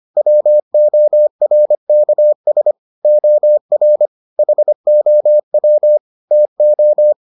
Morse code, heard as a keyed tone, about 25 wpm, 600 Hz. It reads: WORKS OR HOW TO